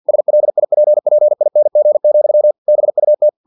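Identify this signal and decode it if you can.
Morse code, heard as a keyed tone, about 50 wpm, 600 Hz. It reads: SLIPPING, BUT
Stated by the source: Morse practice text from an English-language novel at 49 wpm.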